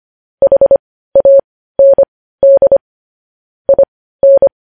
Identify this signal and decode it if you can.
Morse code, heard as a keyed tone, about 25 wpm, spaced as for 15 wpm, 550 Hz. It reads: HAND IN
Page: News headlines in Morse code at 15 wpm